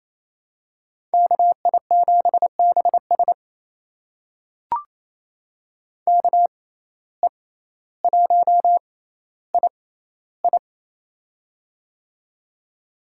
Morse code, heard as a keyed tone, about 28 wpm, spaced as for 14 wpm, 700 Hz. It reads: KI7BS KE1II
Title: USA Call Signs - 5 Characters - 6x Character Spacing - 28wpm